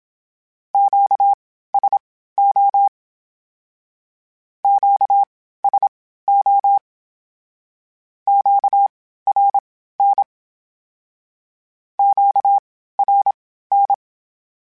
Morse code, heard as a keyed tone, about 27 wpm, spaced as for 15 wpm, 800 Hz. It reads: QSO QSO QRN QRN
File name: (spaced) / Q-Calls Spaced-mp3s / Q Calls Twice sp @25